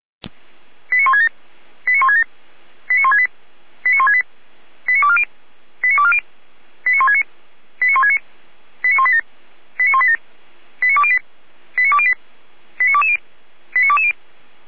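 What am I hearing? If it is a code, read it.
Five-tone selective call (ZVEI), heard as quick runs of five tones, tones of about 70 ms. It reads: 89167 89167 89168 89168 89240 89240 89169 89169 89187 89187 89208 89208 89220 89220
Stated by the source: Five-tone ZVEI selective call received by SDR.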